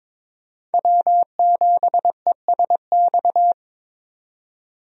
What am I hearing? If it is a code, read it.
Morse code, heard as a keyed tone, about 23 wpm, 700 Hz. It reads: W7ESX
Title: USA Call Signs - 5 Characters - 22wpm